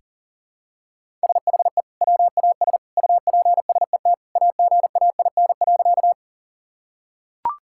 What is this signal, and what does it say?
Morse code, heard as a keyed tone, about 40 wpm, 700 Hz. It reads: SHE WAS UPSET AGAIN.
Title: Sentences from Top 1000 Words - Review Part 5 - 40wpm